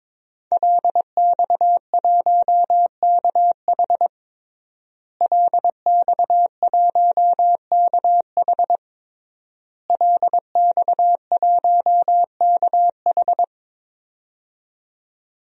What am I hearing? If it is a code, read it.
Morse code, heard as a keyed tone, about 21 wpm, 700 Hz. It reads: LX1KH LX1KH LX1KH
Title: DXCC-100 Call Signs - 5 Characters - Repeated 3x - 22wpm